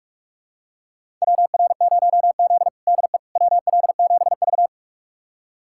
Morse code, heard as a keyed tone, about 44 wpm, 700 Hz. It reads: WR0Z DE WL7V
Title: USA Callsign DE Callsign - 4 Characters - 45wpm